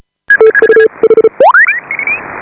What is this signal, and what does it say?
Morse code, heard as a keyed tone, about 36 wpm, 440 Hz. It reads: TU H
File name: AUDIO2018 / HA5BA CW